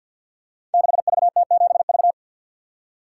Morse code, heard as a keyed tone, about 51 wpm, 700 Hz. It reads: 6VT7V
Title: Code Groups - 5 Letter:Numbers - 50wpm